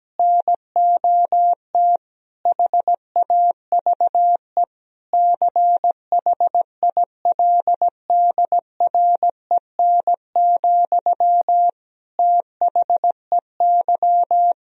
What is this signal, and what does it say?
Morse code, heard as a keyed tone, about 17 wpm, 700 Hz. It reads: NOT HAVE CHILDREN, THEY
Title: Sentences from Top 1000 Words - Review Part 3 - 17wpm